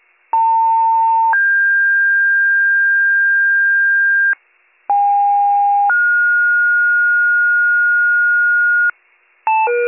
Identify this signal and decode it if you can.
Two-tone page, heard as a one-second tone, then a three-second tone, about 900 then 1600 Hz, about 810 then 1430 Hz, with noise. It is heard from 0.3 to 8.9 s.